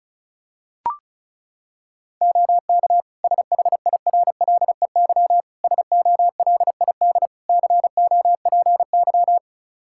Morse code, heard as a keyed tone, about 36 wpm, 700 Hz. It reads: OK SHIRLEY SOLID COPY
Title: Practice QSO - Exchange 3 - More Info - 35wpm